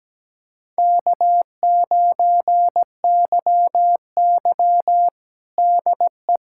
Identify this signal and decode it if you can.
Morse code, heard as a keyed tone, about 17 wpm, 700 Hz. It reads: K9YY DE
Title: USA Callsign DE Callsign - 4 Characters - 17wpm